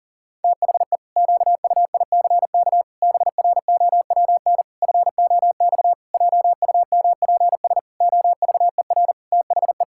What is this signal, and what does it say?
Morse code, heard as a keyed tone, about 37 wpm, 700 Hz. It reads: THE QUICK BROWN FOX JUMPS OVER THE
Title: Pangram Challenge v2 - 40wpm